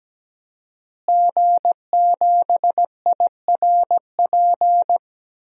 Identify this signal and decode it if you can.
Morse code, heard as a keyed tone, about 17 wpm, 700 Hz. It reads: G7IRP
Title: DXCC-100 Call Signs - 5 Characters - Repeated 3x - 17wpm